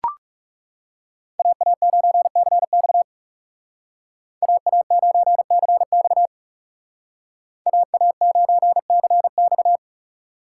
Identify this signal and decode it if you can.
Morse code, heard as a keyed tone, about 36 wpm, 700 Hz. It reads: AA9CX AA9CX AA9CX
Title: Speed Racing - USA 2x2 Call Signs 30wpm